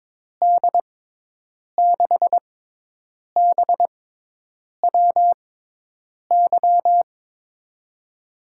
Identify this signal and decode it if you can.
Morse code, heard as a keyed tone, about 22 wpm, spaced as for 8 wpm, 700 Hz. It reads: D6BWY
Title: USA Call Signs - 6 Characters - 6x Character Spacing - 22wpm